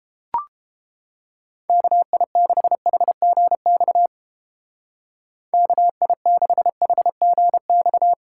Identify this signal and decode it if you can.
Morse code, heard as a keyed tone, about 30 wpm, 700 Hz. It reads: KI6HGX KI6HGX